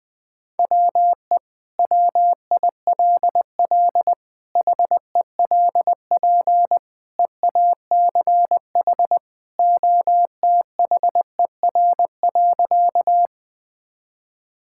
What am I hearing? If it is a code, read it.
Morse code, heard as a keyed tone, about 20 wpm, 700 Hz. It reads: WE WILL HELP EACH OTHER.
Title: Sentences with 2-Word Phrases - Spoken Beginning - 20wpm